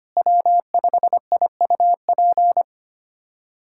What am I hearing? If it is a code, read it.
Morse code, heard as a keyed tone, about 25 wpm, 700 Hz. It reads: W5IUP
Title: USA Callsign DE Callsign - 5 Characters - 25wpm